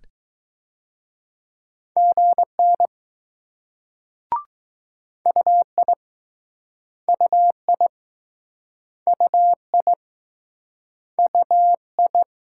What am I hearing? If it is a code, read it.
Morse code, heard as a keyed tone, about 20 wpm, 700 Hz. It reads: GN UI UI UI UI